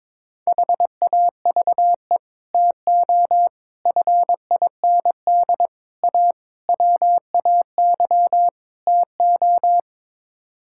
Morse code, heard as a keyed tone, about 22 wpm, 700 Hz. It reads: HAVE TO FIND A WAY TO
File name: practice / 22wpm / Sets of 6 Words - 22wpm